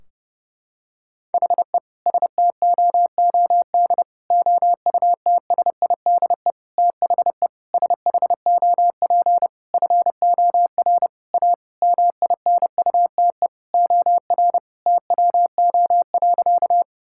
Morse code, heard as a keyed tone, about 30 wpm, 700 Hz. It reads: HE STOOD OUTSIDE THE SHOP FOR A MINUTE OR TWO.
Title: Sentences from Top 1600 Words 30wpm